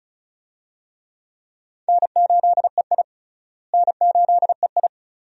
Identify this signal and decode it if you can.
Morse code, heard as a keyed tone, about 36 wpm, 700 Hz. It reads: N8EI N8EI
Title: USA Call Signs - 4 Characters - Repeated 2x - 35wpm